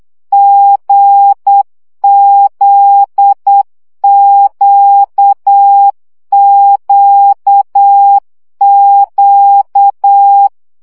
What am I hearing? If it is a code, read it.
Morse code, heard as a keyed tone, about 8 wpm, 800 Hz. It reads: GZQQQ
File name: CW gyakorlas g z q 7 baud